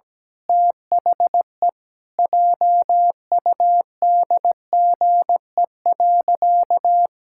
Morse code, heard as a keyed tone, about 17 wpm, 700 Hz. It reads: THE JUDGE.
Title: Pangram Challenge v2 - 17wpm